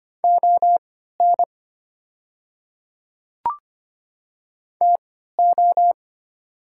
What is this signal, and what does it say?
Morse code, heard as a keyed tone, about 25 wpm, spaced as for 14 wpm, 700 Hz. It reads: ON TO
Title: 3x Character Spacing - Top 100 Words - 25wpm